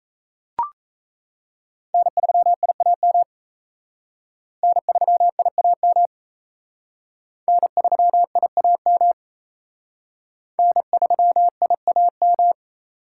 Morse code, heard as a keyed tone, about 35 wpm, 700 Hz. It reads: N3IAM N3IAM N3IAM N3IAM